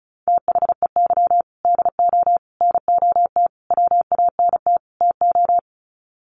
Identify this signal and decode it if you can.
Morse code, heard as a keyed tone, about 34 wpm, 700 Hz. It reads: THEY DO NOT WANT TO
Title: Sets of 5 Words 35wpm